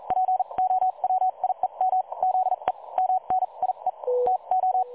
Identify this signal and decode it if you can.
Morse code, heard as a keyed tone, about 38 wpm, spaced as for 32 wpm, 750 Hz. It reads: POWIEM ZE MNIE TO
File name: dlazaawansowanych / nagraniazpasma / websdr recording 3565.0kHz